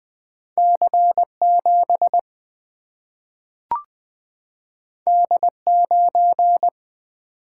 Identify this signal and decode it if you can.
Morse code, heard as a keyed tone, about 20 wpm, 700 Hz. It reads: C7 D9
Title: Single-Letter Single-Number 20wpm